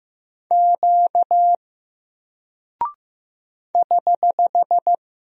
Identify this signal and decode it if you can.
Morse code, heard as a keyed tone, about 15 wpm, 700 Hz. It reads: Q <HH>
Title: Single Character (Letter:Number:Punctuation:Prosign) Alt 15wpm